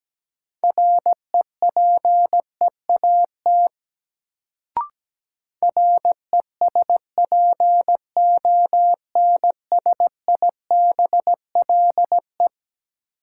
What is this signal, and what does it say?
Morse code, heard as a keyed tone, about 17 wpm, 700 Hz. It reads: REPEAT RESPONSIBLE
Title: Sentences from Top 1600 Words - New Word Drill Alt - 17wpm